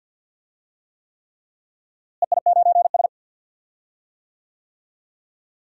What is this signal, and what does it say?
Morse code, heard as a keyed tone, about 49 wpm, 700 Hz. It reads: EI9S
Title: DXCC-100 Call Signs - 4 Characters - Repeated 3x - 50wpm